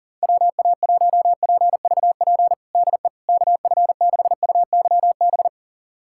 Morse code, heard as a keyed tone, about 40 wpm, 700 Hz. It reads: WA1PVP DE KF6UYB
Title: USA Callsign DE Callsign - 6 Characters - 40wpm